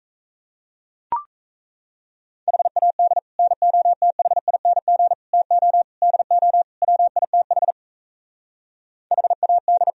Morse code, heard as a keyed tone, about 36 wpm, 700 Hz. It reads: HAD NOTHING TO DO WITH HAD